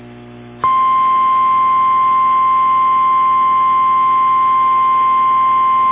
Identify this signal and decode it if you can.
Emergency Alert System (SAME) tones (the 1050 Hz attention tone). It is heard from 1.2 to 5.3 s.
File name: eas / KWBE / KWBE